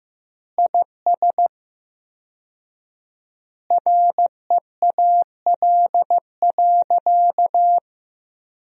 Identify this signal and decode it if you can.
Morse code, heard as a keyed tone, about 15 wpm, 700 Hz. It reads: IS REAL.
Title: 4x Word-Spacing - Sentences from Top 300 Words - 15wpm